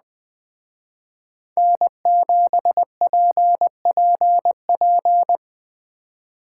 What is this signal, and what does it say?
Morse code, heard as a keyed tone, about 20 wpm, 700 Hz. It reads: N7PPP
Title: Speed Racing - USA 1x3 Call Signs 17wpm